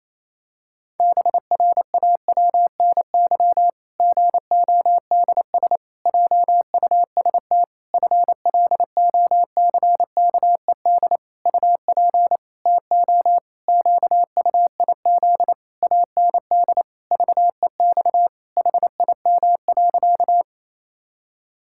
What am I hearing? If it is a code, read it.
Morse code, heard as a keyed tone, about 28 wpm, 700 Hz. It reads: BRAWNY GODS JUST FLOCKED UP TO QUIZ AND VEX HIM.
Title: Pangram Challenge v2 - 28wpm